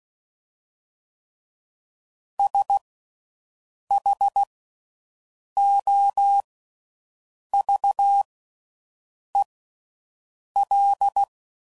Morse code, heard as a keyed tone, about 16 wpm, spaced as for 6 wpm, 800 Hz. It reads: SHOVEL
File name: Slow With Faster Ings